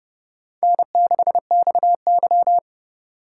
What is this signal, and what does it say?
Morse code, heard as a keyed tone, about 28 wpm, 700 Hz. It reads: N6XY